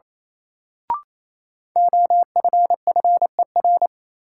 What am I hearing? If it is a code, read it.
Morse code, heard as a keyed tone, about 28 wpm, 700 Hz. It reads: OFFER
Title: Sentences from Top 900 Words - New Word Drill - 28wpm